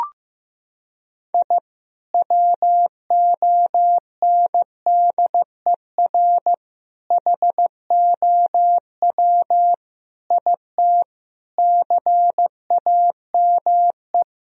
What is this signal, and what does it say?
Morse code, heard as a keyed tone, about 15 wpm, 700 Hz. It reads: I WONDER HOW IT CAME
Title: Sentences from Top 1000 Words - Review Part 3 - 15wpm